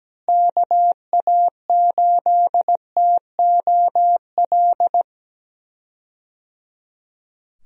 Morse code, heard as a keyed tone, about 17 wpm, 700 Hz. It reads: KA8TOL
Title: USA Call Signs - 6 Characters - Repeated 3x - 17wpm